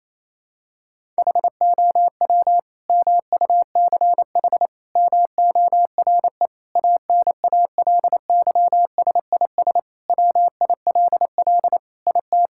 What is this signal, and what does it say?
Morse code, heard as a keyed tone, about 28 wpm, 700 Hz. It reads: HOW MUCH MORE ANALYSIS WILL IT
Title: Sentences from Top 1000 Words - Review Part 5 - 28wpm